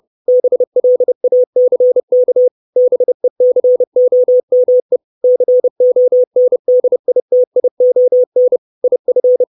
Morse code, heard as a keyed tone, about 30 wpm, 500 Hz. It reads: BLACK BECOME CONDITION IF
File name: 632 Words 30WPM